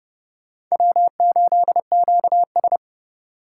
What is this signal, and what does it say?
Morse code, heard as a keyed tone, about 39 wpm, 700 Hz. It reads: W8QS